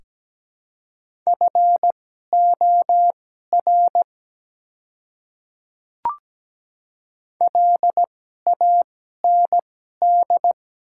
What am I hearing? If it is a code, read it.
Morse code, heard as a keyed tone, about 17 wpm, spaced as for 12 wpm, 700 Hz. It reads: FOR LAND
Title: 2x Character Spacing - Top 200 Words - 17wpm